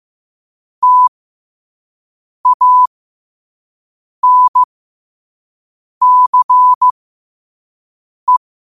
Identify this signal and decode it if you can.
Morse code, heard as a keyed tone, about 15 wpm, spaced as for 5 wpm, 1000 Hz. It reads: TANCE